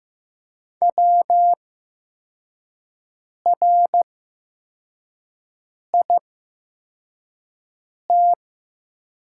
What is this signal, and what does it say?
Morse code, heard as a keyed tone, about 15 wpm, spaced as for 4 wpm, 700 Hz. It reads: WRIT
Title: 8x Character Spacing - Top 100 Words - 15wpm